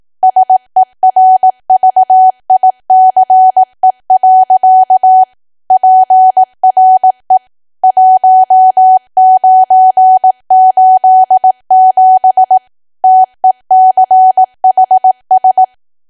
Morse code, heard as a keyed tone, about 18 wpm, 750 Hz. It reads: SERVICE. PRE 1987 TECHS
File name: cwa / W1AW-Level2 / 18WPM